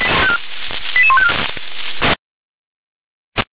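Five-tone selective call (ZVEI), heard as quick runs of five tones, tones of about 70 ms. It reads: 88154
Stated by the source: Five-tone ZVEI selective call received on SDR.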